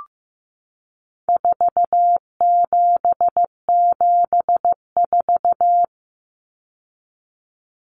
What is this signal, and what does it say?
Morse code, heard as a keyed tone, about 15 wpm, 700 Hz. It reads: 4774
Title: Common Cut Numbers - 4-Digit 15wpm